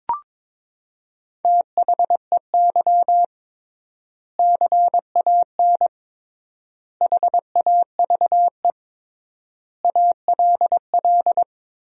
Morse code, heard as a keyed tone, about 22 wpm, 700 Hz. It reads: THEY CAN HAVE ALL
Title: 3x Word-Spacing - Sentences from Top 400 Words - 22wpm